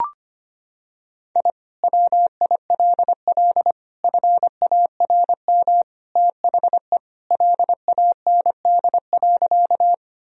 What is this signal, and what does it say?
Morse code, heard as a keyed tone, about 25 wpm, 700 Hz. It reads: I WILL FARM THE LAND.
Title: Sentences from Top 1000 Words - Review Part 2 - 25wpm